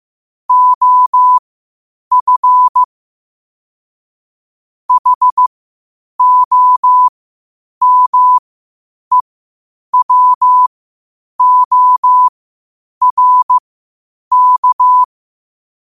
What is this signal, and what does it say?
Morse code, heard as a keyed tone, about 15 wpm, spaced as for 8 wpm, 1000 Hz. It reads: OF HOMEWORK
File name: qotdfiles / 8wpm